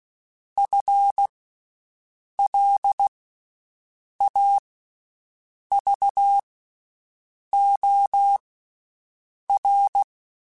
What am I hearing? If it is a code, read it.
Morse code, heard as a keyed tone, about 16 wpm, spaced as for 6 wpm, 800 Hz. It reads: FLAVOR
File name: Special INGs / Special INGs-mp3s / VerySlowToFaster Ing @15